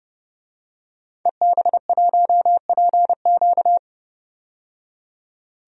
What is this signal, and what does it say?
Morse code, heard as a keyed tone, about 30 wpm, 700 Hz. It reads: EB1PQ
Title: DXCC-100 Call Signs - 5 Characters - 30wpm